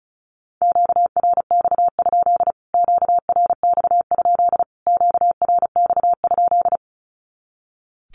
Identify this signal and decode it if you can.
Morse code, heard as a keyed tone, about 34 wpm, 700 Hz. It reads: QRX? QRX? QRX?
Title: Q-Codes Repeated 3x 35wpm